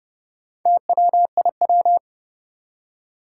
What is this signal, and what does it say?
Morse code, heard as a keyed tone, about 30 wpm, 700 Hz. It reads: TWIW
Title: Four Challenging Letter-Number Combinations 30wpm